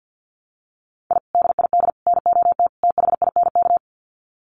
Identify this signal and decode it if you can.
Morse code, heard as a keyed tone, about 48 wpm, 700 Hz. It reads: I DID NOT THINK
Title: Sets of 4 Words 50wpm